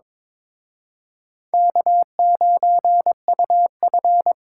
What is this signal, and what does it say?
Morse code, heard as a keyed tone, about 26 wpm, 700 Hz. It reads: K9UF
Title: Speed Racing - USA 1x2 Call Signs 22wpm